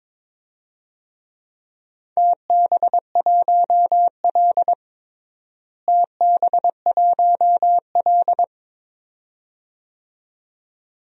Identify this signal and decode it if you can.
Morse code, heard as a keyed tone, about 22 wpm, 700 Hz. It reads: TB1L TB1L
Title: DXCC-100 Call Signs - 4 Characters - Repeated 2x - 22wpm